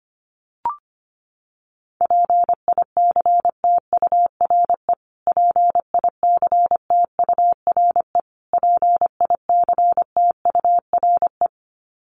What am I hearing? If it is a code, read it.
Morse code, heard as a keyed tone, about 25 wpm, 700 Hz. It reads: PICTURE PICTURE PICTURE
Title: Top 500 Words Repeated 25wpm